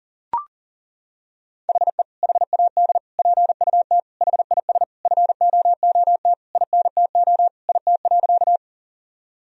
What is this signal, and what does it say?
Morse code, heard as a keyed tone, about 37 wpm, 700 Hz. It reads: HE HAD PUT HIS FOOT INTO IT.
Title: Sentences from Top 1000 Words - Review Part 3 - 40wpm